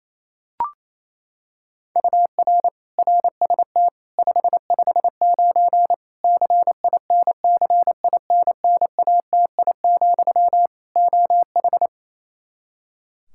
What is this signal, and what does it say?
Morse code, heard as a keyed tone, about 28 wpm, 700 Hz. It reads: UR RST 559 CINCINNATI, OH
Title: Practice QSO - Exchange 1 - Beginning - 28wpm